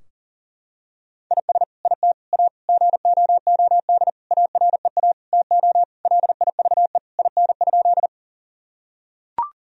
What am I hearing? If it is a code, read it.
Morse code, heard as a keyed tone, about 37 wpm, 700 Hz. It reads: IS IT A GOOD AREA TO LIVE IN?
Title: Sentences from Top 1000 Words - Review Part 2 - 40wpm